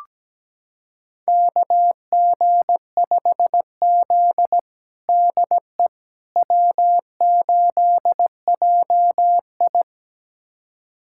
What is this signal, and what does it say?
Morse code, heard as a keyed tone, about 17 wpm, 700 Hz. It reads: KG5Z DE W8JI
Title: USA Callsign DE Callsign - 4 Characters - 17wpm